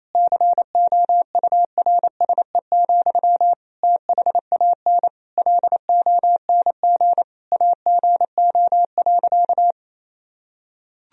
Morse code, heard as a keyed tone, about 28 wpm, 700 Hz. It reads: COURSE, THAN LONG AGO.
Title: Sentences from Top 1000 Words - Review Part 3 - 28wpm